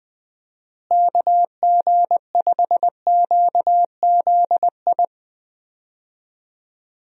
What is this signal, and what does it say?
Morse code, heard as a keyed tone, about 20 wpm, 700 Hz. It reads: KG5QZI
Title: USA Call Signs - 6 Characters - Repeated 3x - 20wpm